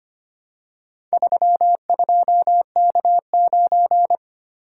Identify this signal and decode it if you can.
Morse code, heard as a keyed tone, about 25 wpm, 700 Hz. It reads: 32K9
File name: practice / 25wpm / Four Letter-Number Combinations - 25wpm